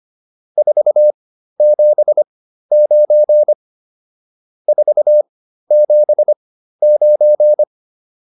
Morse code, heard as a keyed tone, about 25 wpm, spaced as for 13 wpm, 600 Hz. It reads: 479 479